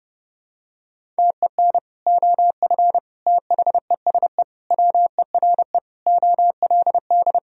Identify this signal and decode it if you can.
Morse code, heard as a keyed tone, about 30 wpm, 700 Hz. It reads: TEN OF THESE WERE OLD
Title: Sentences from Top 1000 Words - Review Part 2 - 30wpm